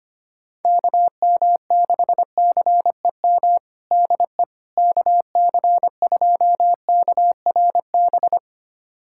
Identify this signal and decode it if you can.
Morse code, heard as a keyed tone, about 25 wpm, 700 Hz. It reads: KM6CEM DE KC2KRB
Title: USA Callsign DE Callsign - 6 Characters - 25wpm